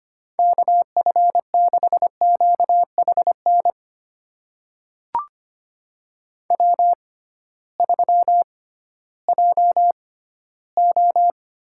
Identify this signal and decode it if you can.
Morse code, heard as a keyed tone, about 25 wpm, spaced as for 14 wpm, 700 Hz. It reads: KF6QHN W3JO
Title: USA Call Signs - 6x Character Spacing - 25wpm